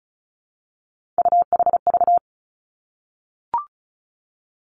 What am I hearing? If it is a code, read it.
Morse code, heard as a keyed tone, about 36 wpm, 700 Hz. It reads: UHV
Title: Three Challenging Letter-Number Combinations 35wpm